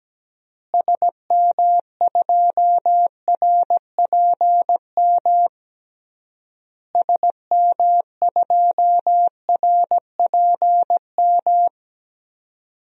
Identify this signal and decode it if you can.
Morse code, heard as a keyed tone, about 17 wpm, 700 Hz. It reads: SM2RPM SM2RPM
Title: DXCC-100 Call Signs - 6 Characters - Repeated 3x - 17wpm